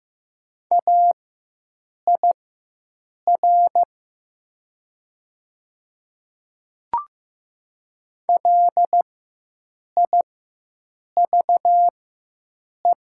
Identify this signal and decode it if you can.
Morse code, heard as a keyed tone, about 15 wpm, spaced as for 7 wpm, 700 Hz. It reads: AIR LIVE
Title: 4x Character Spacing - Top 200 Words - 15wpm